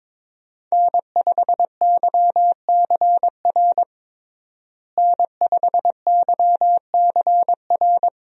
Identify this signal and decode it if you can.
Morse code, heard as a keyed tone, about 21 wpm, 700 Hz. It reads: N5YCR N5YCR
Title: USA Call Signs - 5 Characters - Repeated 2x - 22wpm